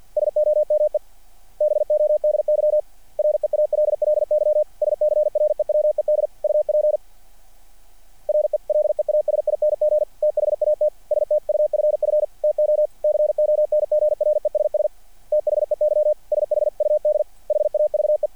Morse code, heard as a keyed tone, about 49 wpm, 600 Hz. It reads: SOMEBODY REALLY SCREWED UP RELEASING THAT STUFF TO CONGRESS THEY SHUD HAVE
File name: archived / CRSnet